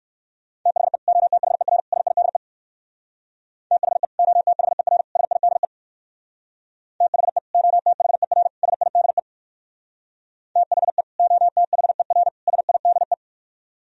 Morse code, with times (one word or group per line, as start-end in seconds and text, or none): 0.7-1.0 THE
1.1-1.8 OTHER
1.9-2.4 SIDE
3.7-4.0 THE
4.2-5.0 OTHER
5.2-5.7 SIDE
7.0-7.4 THE
7.5-8.5 OTHER
8.6-9.2 SIDE
10.6-11.0 THE
11.2-12.3 OTHER
12.5-13.1 SIDE